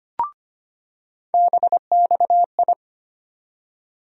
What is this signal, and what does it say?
Morse code, heard as a keyed tone, about 25 wpm, 700 Hz. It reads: BXI
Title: USA Call Sign Postfixes - 3 Letters 25wpm